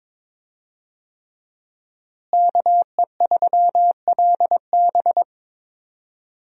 Morse code, heard as a keyed tone, about 22 wpm, 700 Hz. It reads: KE3LB
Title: USA Call Signs - 5 Characters - Repeated 3x - 22wpm